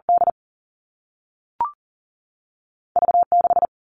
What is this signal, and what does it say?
Morse code, heard as a keyed tone, about 40 wpm, 700 Hz. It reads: D V 6
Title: Two Challenging Letter-Number Combinations 40wpm